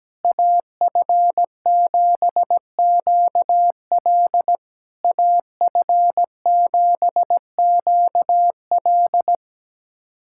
Morse code, heard as a keyed tone, about 17 wpm, 700 Hz. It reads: AF7QL AF7QL
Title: SOTA QSO Practice 17wpm